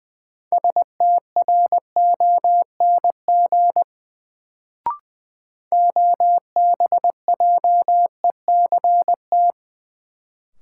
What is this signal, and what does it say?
Morse code, heard as a keyed tone, about 20 wpm, 700 Hz. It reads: STRONG OBJECT
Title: Sentences from Top 500 Words - New Word Drill - 20wpm